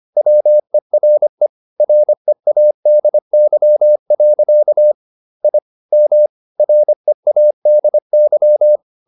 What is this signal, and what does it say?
Morse code, heard as a keyed tone, about 25 wpm, 600 Hz. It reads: WERE READY. I M READY
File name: the old man and the sea part 1 25wpm